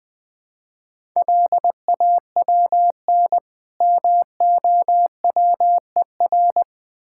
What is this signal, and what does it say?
Morse code, heard as a keyed tone, about 22 wpm, 700 Hz. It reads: LAWN MOWER
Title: Easy Words - Speed-Racing with Hints - 20wpm